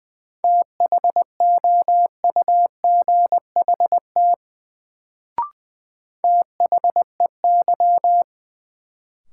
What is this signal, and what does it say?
Morse code, heard as a keyed tone, about 20 wpm, 700 Hz. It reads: THOUGHT THEY
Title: Sets of 2 Words - New Word Drill - 20wpm